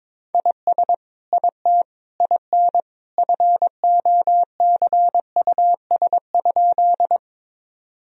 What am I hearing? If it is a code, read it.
Morse code, heard as a keyed tone, about 22 wpm, 700 Hz. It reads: IS IT IN FOCUS?
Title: Sentences from Top 1000 Words - Review Part 4 - 22wpm